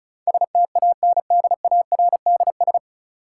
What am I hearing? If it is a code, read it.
Morse code, heard as a keyed tone, about 36 wpm, 700 Hz. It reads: STANDARDS
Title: Sentences from Top 1000 Words - Review Part 6 - New Word Drill - 35wpm